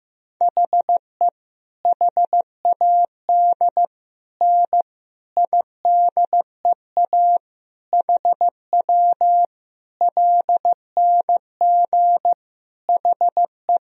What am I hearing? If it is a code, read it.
Morse code, heard as a keyed tone, about 15 wpm, 700 Hz. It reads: HE HAD N IDEA HW LNG HE
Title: Sentences from Top 300 Words and Abbreviations 15wpm